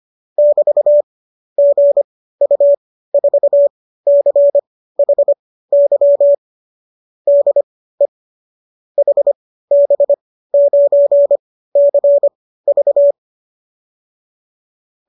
Morse code, heard as a keyed tone, about 25 wpm, spaced as for 15 wpm, 590 Hz. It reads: <BT>GU4CHY DE HB9CV